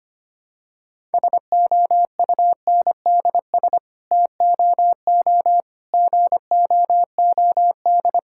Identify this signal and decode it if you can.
Morse code, heard as a keyed tone, about 25 wpm, 700 Hz. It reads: SOUNDS TOO GOOD
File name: Sets of 9 Words - 25wpm